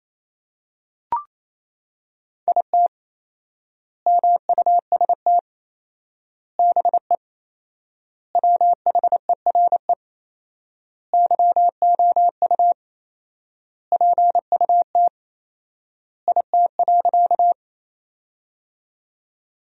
Morse code, heard as a keyed tone, about 28 wpm, 700 Hz. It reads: IT MUST BE WHERE YOU PUT IT.
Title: 4x Word-Spacing - Sentences from Top 200 Words - 28wpm